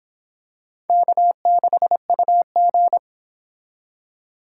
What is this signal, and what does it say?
Morse code, heard as a keyed tone, about 25 wpm, 700 Hz. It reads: K6UG